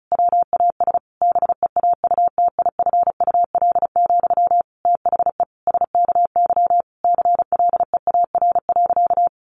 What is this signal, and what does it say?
Morse code, heard as a keyed tone, about 34 wpm, 700 Hz. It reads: WAS BEAUTIFUL, THE SKY CLEAR.
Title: Sentences from Top 1200 Words 35wpm